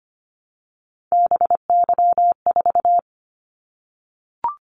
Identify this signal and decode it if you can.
Morse code, heard as a keyed tone, about 25 wpm, 700 Hz. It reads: BY4
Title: Callsign Prefixes for Top 100 DXCC - 25wpm